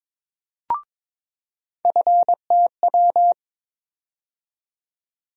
Morse code, heard as a keyed tone, about 22 wpm, 700 Hz. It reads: FTW
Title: USA Call Sign Postfixes - 3 Letters 22wpm